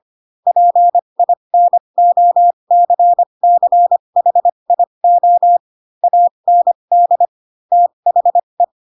Morse code, heard as a keyed tone, about 25 wpm, spaced as for 22 wpm, 700 Hz. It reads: PINOCCHIO AND THE
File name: Pinocchio / Pinocchio ch4 18 25 ik1hge